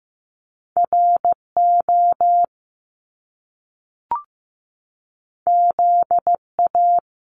Morse code, heard as a keyed tone, about 15 wpm, 700 Hz. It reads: RO ZA
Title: Top Two Letter Combinations in Words 15wpm